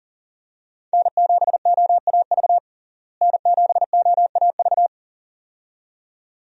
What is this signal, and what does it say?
Morse code, heard as a keyed tone, about 40 wpm, 700 Hz. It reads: N7OAV N7OAV
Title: USA Call Signs - 5 Characters - Repeated 2x - 40wpm